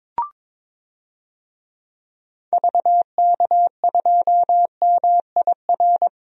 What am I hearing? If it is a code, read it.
Morse code, heard as a keyed tone, about 22 wpm, 700 Hz. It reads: VK2MIR
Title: DXCC-100 Call Signs - 6 Characters - Repeated 2x - 22wpm